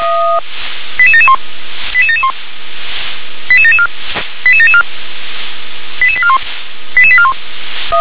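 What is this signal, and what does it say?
Five-tone selective call (ZVEI), heard as quick runs of five tones, tones of about 70 ms. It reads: 88771 88771 88774 88774 88841 88841
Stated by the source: ZVEI five-tone selective call 88979, received via SDR.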